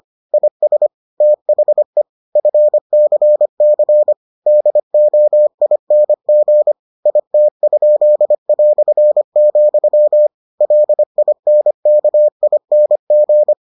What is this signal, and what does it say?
Morse code, heard as a keyed tone, about 25 wpm, 600 Hz. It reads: IS THE FCC DOING IT?", LINKING